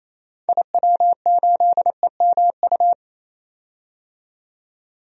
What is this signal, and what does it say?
Morse code, heard as a keyed tone, about 28 wpm, 700 Hz. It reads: IW8EMU
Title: DXCC-100 Call Signs - 6 Characters - 28wpm